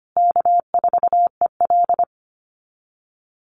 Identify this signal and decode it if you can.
Morse code, heard as a keyed tone, about 25 wpm, 700 Hz. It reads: K4EL
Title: USA Call Signs Repeated 3x 25wpm